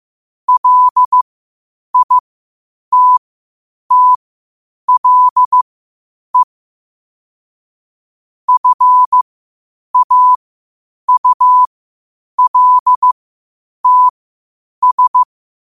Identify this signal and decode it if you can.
Morse code, heard as a keyed tone, about 15 wpm, spaced as for 8 wpm, 1000 Hz. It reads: LITTLE FAULTS
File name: qotdfiles / 8wpm